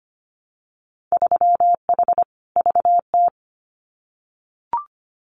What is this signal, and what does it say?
Morse code, heard as a keyed tone, about 25 wpm, 690 Hz. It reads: 3H VT
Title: Winter Field Day Exchanges Encore 25wpm